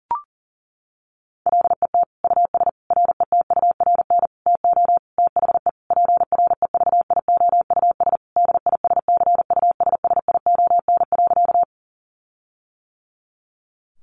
Morse code, heard as a keyed tone, about 40 wpm, 700 Hz. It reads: LET US RETURN TO THE PREVIOUS DISCUSSION.
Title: Sentences from Top 1500 Words 40wpm